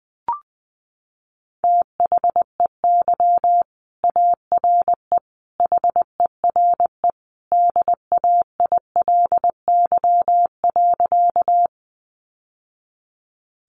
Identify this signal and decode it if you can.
Morse code, heard as a keyed tone, about 20 wpm, 700 Hz. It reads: THEY ARE HERE DAILY.
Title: Sentences from Top 1100 Words 20wpm